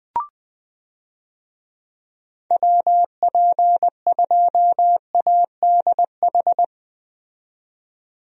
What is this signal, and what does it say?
Morse code, heard as a keyed tone, about 20 wpm, 700 Hz. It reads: WP2ADH
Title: USA Call Signs - 6 Characters - 20wpm